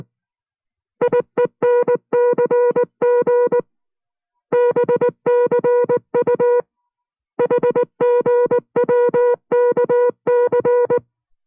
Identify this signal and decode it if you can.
Morse code, heard as a keyed tone, about 19 wpm, 490 Hz. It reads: IENCG BCU HGWKC